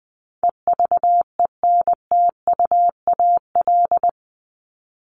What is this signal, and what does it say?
Morse code, heard as a keyed tone, about 20 wpm, 700 Hz. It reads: EVENTUAL
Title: LY Suffix - Top 100 Words with Base Words 20wpm